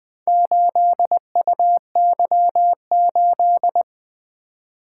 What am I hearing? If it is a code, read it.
Morse code, heard as a keyed tone, about 20 wpm, 700 Hz. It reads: 8UY8
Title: Four Letter-Number Combinations - 20wpm